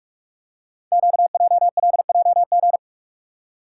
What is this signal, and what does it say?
Morse code, heard as a keyed tone, about 44 wpm, 700 Hz. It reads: QJLJG
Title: Code Groups - 5 Letters - 45wpm